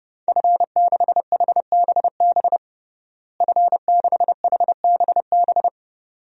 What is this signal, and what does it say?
Morse code, heard as a keyed tone, about 30 wpm, 700 Hz. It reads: F6HBB F6HBB
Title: DXCC-100 Call Signs - 5 Characters - Repeated 2x - 30wpm